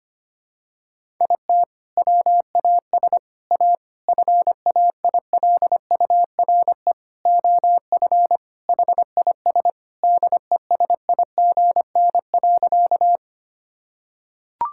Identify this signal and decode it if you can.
Morse code, heard as a keyed tone, about 25 wpm, 700 Hz. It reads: IT WAS A FAILURE OF HIS DESIGN.
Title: Sentences from Top 1000 Words - Review Part 6 - 25wpm